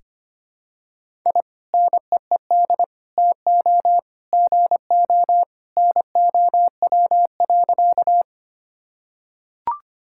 Morse code, heard as a keyed tone, about 25 wpm, 700 Hz. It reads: I NEED TO GO NOW.
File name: practice / 25wpm / Sentences with 2-Word Phrases - 25wpm